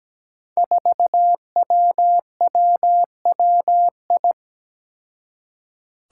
Morse code, heard as a keyed tone, about 17 wpm, 700 Hz. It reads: 4WWWI
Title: Five Challenging Letter-Number Combinations 17wpm